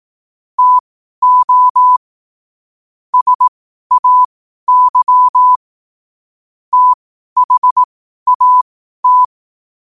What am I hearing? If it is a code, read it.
Morse code, heard as a keyed tone, about 18 wpm, spaced as for 12 wpm, 1000 Hz. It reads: TO SAY THAT
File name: qotdfiles / 12wpm